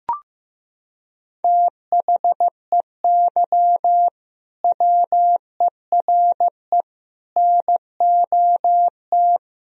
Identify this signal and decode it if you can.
Morse code, heard as a keyed tone, about 15 wpm, 700 Hz. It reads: THEY WERE NOT
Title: Sentences from Top 1000 Words - Review Part 3 - 15wpm